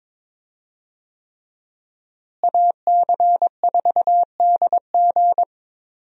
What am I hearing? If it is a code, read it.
Morse code, heard as a keyed tone, about 22 wpm, 700 Hz. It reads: AC4DG
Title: USA Call Signs - 5 Characters - Repeated 3x - 22wpm